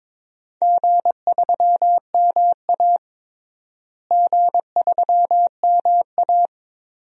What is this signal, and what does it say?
Morse code, heard as a keyed tone, about 22 wpm, 700 Hz. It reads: G3MA G3MA
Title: DXCC-100 Call Signs - 4 Characters - Repeated 2x - 22wpm